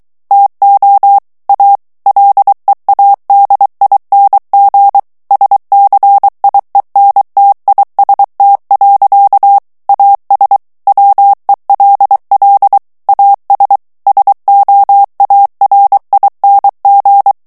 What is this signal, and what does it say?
Morse code, heard as a keyed tone, about 23 wpm, 800 Hz. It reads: TO A LEADING SCIENTIST. AS WELL AS SOARING
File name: morse mp3 / ANGLAIS texte clair 2 vitesse 22 mots minute